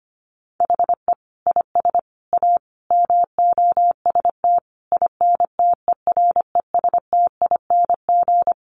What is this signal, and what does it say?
Morse code, heard as a keyed tone, about 25 wpm, 700 Hz. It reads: HE IS A MOST INTERESTING
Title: Sentences from Top 1000 Words 25wpm